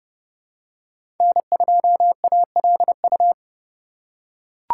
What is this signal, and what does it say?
Morse code, heard as a keyed tone, about 29 wpm, 700 Hz. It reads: N2ALU
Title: USA Call Signs - 5 Characters - 4x Character Spacing - 30wpm